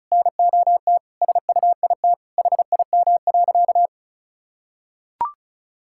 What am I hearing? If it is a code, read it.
Morse code, heard as a keyed tone, about 36 wpm, 700 Hz. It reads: NOT SUIT HIM.
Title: Sentences from Top 1600 Words 35wpm